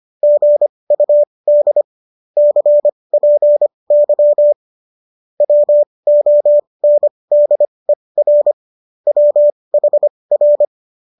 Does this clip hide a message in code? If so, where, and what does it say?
Yes: Morse code from 0.2 to 10.7 s, heard as a keyed tone, about 25 wpm, spaced as for 20 wpm, 590 Hz. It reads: GUD CPY WONDER WHR